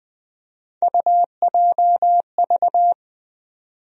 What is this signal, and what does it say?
Morse code, heard as a keyed tone, about 20 wpm, 700 Hz. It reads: UJV
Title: USA Call Sign Postfixes - 3 Letters 20wpm